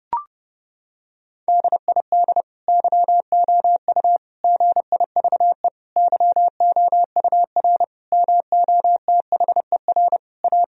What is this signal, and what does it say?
Morse code, heard as a keyed tone, about 30 wpm, 700 Hz. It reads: DID YOU GIVE YOUR MOTHER A
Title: Sentences from Top 1000 Words - Review Part 4 - 30wpm